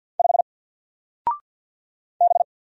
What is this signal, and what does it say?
Morse code, heard as a keyed tone, about 48 wpm, 700 Hz. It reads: 5 B
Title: Single Letter-Number Alt 50wpm